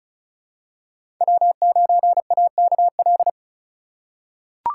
Morse code, heard as a keyed tone, about 36 wpm, 700 Hz. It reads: W9AKL
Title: USA Call Signs - 2x Character Spacing - 35wpm